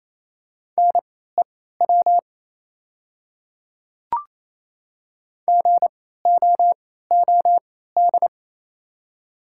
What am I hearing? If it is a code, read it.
Morse code, heard as a keyed tone, about 28 wpm, spaced as for 16 wpm, 700 Hz. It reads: NEW GOOD
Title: 3x Character Spacing - Top 300 Words - 28wpm